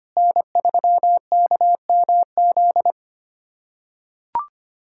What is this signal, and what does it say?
Morse code, heard as a keyed tone, about 25 wpm, 700 Hz. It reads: N3KMZ
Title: USA Callsign DE Callsign - 5 Characters - 25wpm